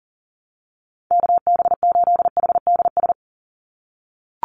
Morse code, heard as a keyed tone, about 40 wpm, 700 Hz. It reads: KB8HDS
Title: USA Call Signs 40wpm